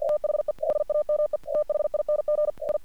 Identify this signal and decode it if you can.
Morse code, heard as a keyed tone, about 53 wpm, 600 Hz. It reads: THE SAME THING I
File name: archived / CRSnet